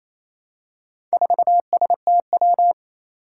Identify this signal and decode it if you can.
Morse code, heard as a keyed tone, about 28 wpm, 700 Hz. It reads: 4STW
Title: Four Challenging Letter-Number Combinations 28wpm